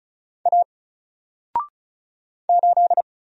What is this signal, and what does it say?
Morse code, heard as a keyed tone, about 35 wpm, 700 Hz. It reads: A 8
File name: Single-Letter-Number-35wpm